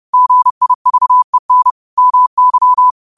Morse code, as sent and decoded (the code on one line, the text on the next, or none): --. .. ...- . -. / -- -.--
GIVEN MY